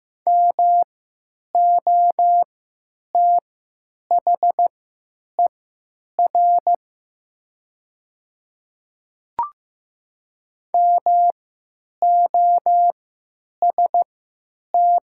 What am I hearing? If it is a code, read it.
Morse code, heard as a keyed tone, about 15 wpm, spaced as for 9 wpm, 700 Hz. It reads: MOTHER MOST
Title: 3x Character Spacing - Top 300 Words - 15wpm